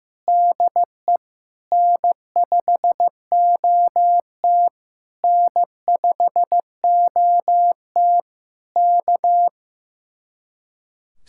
Morse code, heard as a keyed tone, about 15 wpm, 700 Hz. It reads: DE N5OT N5OT K
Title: Speed-Racing Common QSO Phrases Encore - 15wpm